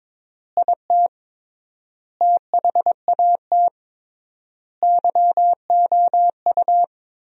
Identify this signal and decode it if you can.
Morse code, heard as a keyed tone, about 22 wpm, 700 Hz. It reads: IT THAT YOU
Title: 3x Word-Spacing - Sentences from Top 200 Words - 22wpm